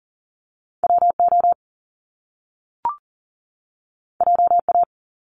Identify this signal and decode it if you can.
Morse code, heard as a keyed tone, about 40 wpm, 700 Hz. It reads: WO JA
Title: Top Two Letter Combinations in Words 40wpm